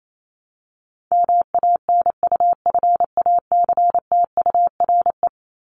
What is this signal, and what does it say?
Morse code, heard as a keyed tone, about 28 wpm, 700 Hz. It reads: MANUFACTURE
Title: ING Suffix - Top 100 Words with Base Words 28wpm